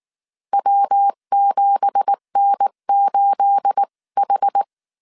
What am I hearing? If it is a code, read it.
Morse code, heard as a keyed tone, about 19 wpm, 800 Hz. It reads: W7N8H
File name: CW / DET / 1 9 1500C S B 18wpm 2551